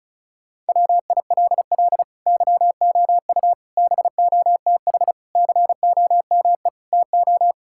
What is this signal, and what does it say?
Morse code, heard as a keyed tone, about 36 wpm, 700 Hz. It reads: WILL YOU BOTH COME TO
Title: Sentences from Top 1600 Words 35wpm